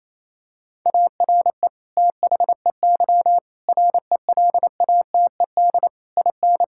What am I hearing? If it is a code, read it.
Morse code, heard as a keyed tone, about 28 wpm, 700 Hz. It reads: ARE THEY RELATED IN
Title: Sentences from Top 1000 Words - Review Part 5 - 28wpm